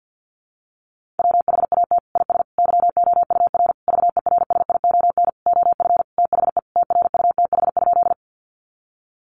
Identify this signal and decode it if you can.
Morse code, heard as a keyed tone, about 50 wpm, 700 Hz. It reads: WHAT IS YOUR VERSION OF THE TRUTH?
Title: Sentences from Top 1400 Words 50wpm